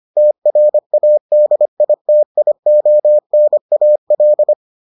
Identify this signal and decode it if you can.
Morse code, heard as a keyed tone, about 25 wpm, 600 Hz. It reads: TRADITIONAL